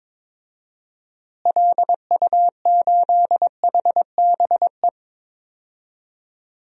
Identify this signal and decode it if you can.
Morse code, heard as a keyed tone, about 22 wpm, 700 Hz. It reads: LU8HBE
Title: DXCC-100 Call Signs - 6 Characters - Repeated 3x - 22wpm